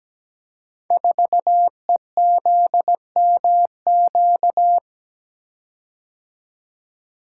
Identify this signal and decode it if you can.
Morse code, heard as a keyed tone, about 17 wpm, 700 Hz. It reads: 4EZMQ
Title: Code Groups - 5 Letter:Numbers - 17wpm